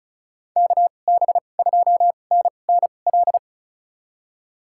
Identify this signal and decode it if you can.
Morse code, heard as a keyed tone, about 36 wpm, spaced as for 25 wpm, 700 Hz. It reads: KB2NNL
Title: USA Call Signs - 2x Character Spacing - 35wpm